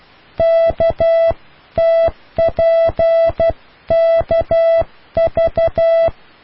Morse code, heard as a keyed tone, about 12 wpm, 660 Hz. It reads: KTPKV